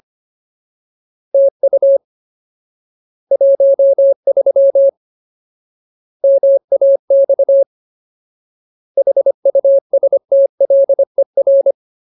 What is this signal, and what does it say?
Morse code, heard as a keyed tone, about 25 wpm, 550 Hz. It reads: TU 13 MAX HUSTLER
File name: Ultimate Word List / 25WPM Ultimate